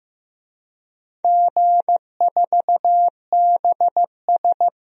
Morse code, heard as a keyed tone, about 15 wpm, 700 Hz. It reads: G4BS
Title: DXCC-100 Call Signs - 4 Characters - Repeated 2x - 15wpm